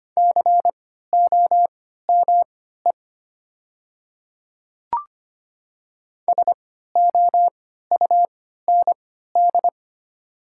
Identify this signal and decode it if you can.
Morse code, heard as a keyed tone, about 25 wpm, spaced as for 14 wpm, 700 Hz. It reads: COME SOUND
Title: 3x Character Spacing - Top 100 Words - 25wpm